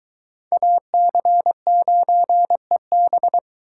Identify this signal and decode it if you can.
Morse code, heard as a keyed tone, about 21 wpm, 700 Hz. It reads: AC9EB